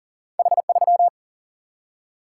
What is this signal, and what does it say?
Morse code, heard as a keyed tone, about 40 wpm, 700 Hz. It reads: H3
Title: Single-Letter Single-Number 40wpm